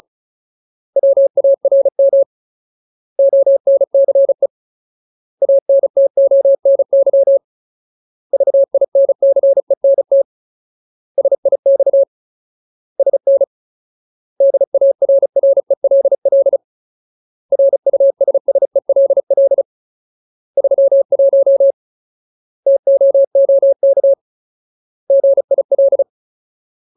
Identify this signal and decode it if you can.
Morse code, heard as a keyed tone, about 37 wpm, 550 Hz. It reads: WARM ONCE ANTONY VINCENT SIX SN DARRELL RUSSELL 31 TOOK GIL
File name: Ultimate Word List / 35WPM Ultimate